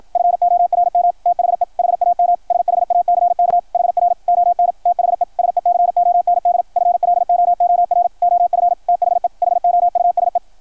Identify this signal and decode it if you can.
Morse code, heard as a keyed tone, about 53 wpm, 690 Hz. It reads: FORM THE HAM SHACK HR ON THE SECOND FLOOR OF THE HOUSE
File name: archived / CRSnet